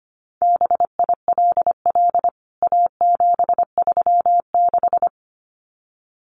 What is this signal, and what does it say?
Morse code, heard as a keyed tone, about 25 wpm, 700 Hz. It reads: BILL A736
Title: CWT Exchange 25wpm